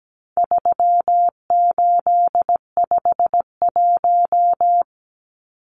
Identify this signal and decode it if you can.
Morse code, heard as a keyed tone, about 17 wpm, 700 Hz. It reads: 3851
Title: Numbers - 4-Digit 17wpm